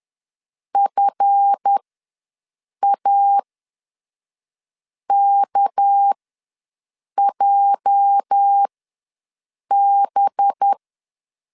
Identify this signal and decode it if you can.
Morse code, heard as a keyed tone, about 10 wpm, spaced as for 5 wpm, 800 Hz. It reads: FAKJB
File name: LEZIONI / 8 YJ45 / 8 YJ ws 10wpm 4043